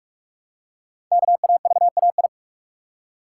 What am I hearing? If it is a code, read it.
Morse code, heard as a keyed tone, about 44 wpm, 700 Hz. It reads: KAVAI
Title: Code Groups - 5 Letters - 45wpm